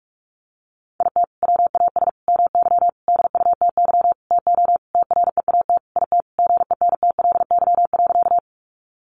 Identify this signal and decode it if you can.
Morse code, heard as a keyed tone, about 44 wpm, 700 Hz. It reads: IT WAS MY DUTY TO TREAT IT GENTLY.
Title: Sentences from Top 1500 Words 45wpm